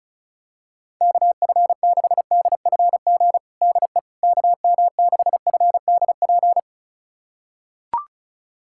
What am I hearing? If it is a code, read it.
Morse code, heard as a keyed tone, about 36 wpm, 700 Hz. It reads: KF6DFG DE KM6FDP
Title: USA Callsign DE Callsign - 6 Characters - 35wpm